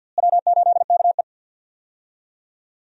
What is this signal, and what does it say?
Morse code, heard as a keyed tone, about 46 wpm, 700 Hz. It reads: W8KE
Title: USA Call Signs - 4 Characters - Repeated 3x - 50wpm